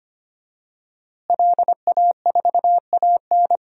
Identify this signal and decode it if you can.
Morse code, heard as a keyed tone, about 25 wpm, 700 Hz. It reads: LA4AN
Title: DXCC-100 Call Signs - 5 Characters - Repeated 2x - 25wpm